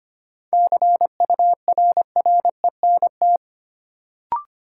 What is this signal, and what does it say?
Morse code, heard as a keyed tone, about 25 wpm, 700 Hz. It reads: CURRENT
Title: Sentences from Top 1000 Words - Review Part 4 - New Word Drill - 25wpm